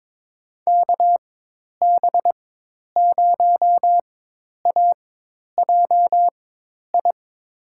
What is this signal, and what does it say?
Morse code, heard as a keyed tone, about 22 wpm, spaced as for 10 wpm, 700 Hz. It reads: KB0AJI
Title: USA Call Signs - 6 Characters - 4x Character Spacing - 22wpm